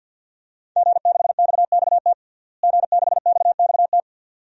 Morse code, heard as a keyed tone, about 50 wpm, 700 Hz. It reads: G6XXT G6XXT
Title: DXCC-100 Call Signs - 5 Characters - Repeated 2x - 50wpm